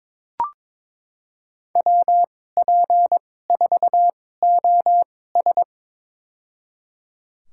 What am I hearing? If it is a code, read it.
Morse code, heard as a keyed tone, about 22 wpm, spaced as for 16 wpm, 700 Hz. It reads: WP4OS
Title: USA Call Signs - 5 Characters - 2x Character Spacing - 22wpm